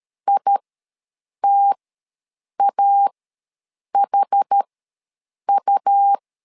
Morse code, heard as a keyed tone, about 13 wpm, spaced as for 7 wpm, 800 Hz. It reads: ITAHU
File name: LEZIONI / 2 UHDN / 2 UHD ws 12wpm 1722